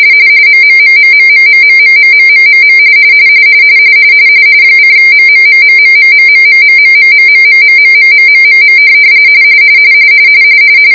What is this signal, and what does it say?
RTTY, about 45.45 baud. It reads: MMMMMMMMMMMMMM :MMMMMMMMMMMMMMMMMMMMMM